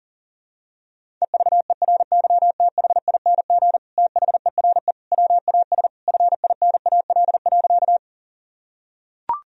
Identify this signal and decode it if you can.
Morse code, heard as a keyed tone, about 40 wpm, 700 Hz. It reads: EVERYTHING THERE WAS FINAL.
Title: Sentences from Top 1000 Words - Review Part 2 - 40wpm